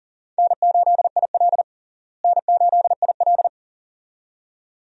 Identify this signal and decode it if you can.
Morse code, heard as a keyed tone, about 40 wpm, 700 Hz. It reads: N8IL N8IL
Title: USA Call Signs - 4 Characters - Repeated 2x - 40wpm